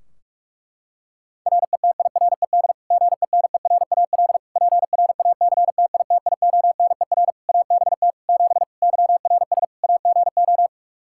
Morse code, heard as a keyed tone, about 46 wpm, 700 Hz. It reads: RETIRED GENERAL PRACTITIONER ABT 7 YRS AGO
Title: Practice QSO - Exchange 3 - More Info - 45wpm